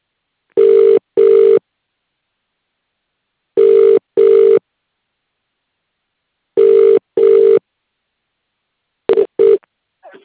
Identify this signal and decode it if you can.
Ringback tone (UK), 400 + 450 Hz.